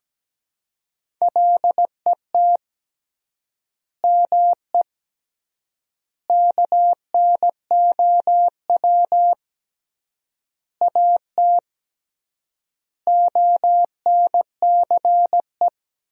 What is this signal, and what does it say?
Morse code, heard as a keyed tone, about 17 wpm, 700 Hz. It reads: LET ME KNOW AT ONCE
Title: 3x Word-Spacing - Sentences from Top 300 Words - 17wpm